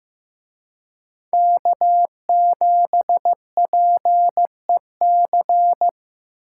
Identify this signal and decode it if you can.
Morse code, heard as a keyed tone, about 18 wpm, 700 Hz. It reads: K7PEC
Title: Speed Racing - USA 1x3 Call Signs 15wpm